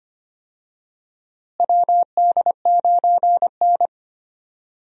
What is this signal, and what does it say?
Morse code, heard as a keyed tone, about 25 wpm, 700 Hz. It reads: WD9N
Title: USA Call Signs - 4 Characters - Repeated 3x - 25wpm